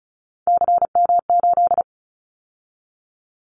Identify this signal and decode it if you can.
Morse code, heard as a keyed tone, about 34 wpm, 700 Hz. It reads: CM8
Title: Callsign Prefixes for Top 100 DXCC - 35wpm